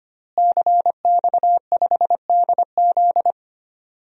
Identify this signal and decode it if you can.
Morse code, heard as a keyed tone, about 25 wpm, 700 Hz. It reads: CX5DZ
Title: DXCC-100 Call Signs - 5 Characters - Repeated 3x - 25wpm